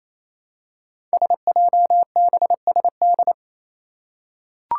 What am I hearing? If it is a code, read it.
Morse code, heard as a keyed tone, about 28 wpm, 700 Hz. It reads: SJBSD
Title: Five Challenging Letter-Number Combinations 28wpm